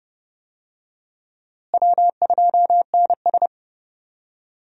Morse code, heard as a keyed tone, about 29 wpm, 700 Hz. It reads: W2NS
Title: USA Call Signs - 4 Characters - Repeated 3x - 30wpm